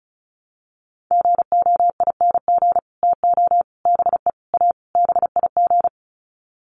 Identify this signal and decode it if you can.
Morse code, heard as a keyed tone, about 34 wpm, 700 Hz. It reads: GOING TO BE A BIG
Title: Sets of 5 Words Encore2 35wpm